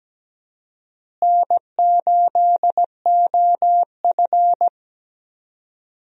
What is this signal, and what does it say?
Morse code, heard as a keyed tone, about 17 wpm, 700 Hz. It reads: N8OF
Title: USA Call Signs - 4 Characters - Repeated 3x - 17wpm